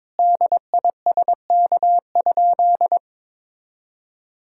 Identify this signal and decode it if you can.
Morse code, heard as a keyed tone, about 22 wpm, 700 Hz. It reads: DISK?
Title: Sentences from Top 1000 Words - Review Part 6 - 22wpm